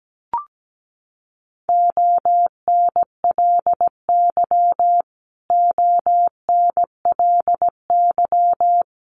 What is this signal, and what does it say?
Morse code, heard as a keyed tone, about 17 wpm, 700 Hz. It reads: ONLY ONLY
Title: Top 500 Words Repeated 17wpm